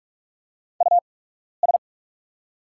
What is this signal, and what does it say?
Morse code, heard as a keyed tone, about 44 wpm, spaced as for 12 wpm, 690 Hz. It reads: US
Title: 8x Character Spacing - Top 200 Words - 45wpm